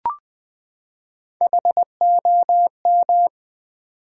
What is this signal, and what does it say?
Morse code, heard as a keyed tone, about 20 wpm, 690 Hz. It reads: HOM
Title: Build A Word - 3 Letters At A Time - 8-Letter Words - 20wpm